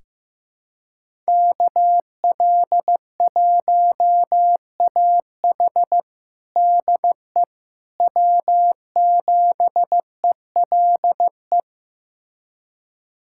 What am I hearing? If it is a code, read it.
Morse code, heard as a keyed tone, about 15 wpm, 700 Hz. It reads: KL1AH DE W7ELE
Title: USA Callsign DE Callsign 15wpm